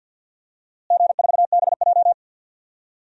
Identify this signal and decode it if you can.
Morse code, heard as a keyed tone, about 51 wpm, 700 Hz. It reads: G4BJ
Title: DXCC-100 Call Signs - 4 Characters - Repeated 2x - 50wpm